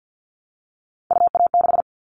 Morse code, heard as a keyed tone, about 50 wpm, 700 Hz. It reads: UA6
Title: Callsign Prefixes for Top 25 DXCC - Repeated 3x - 50wpm